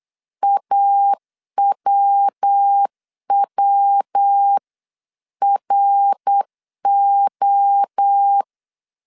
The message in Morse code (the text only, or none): AWWRO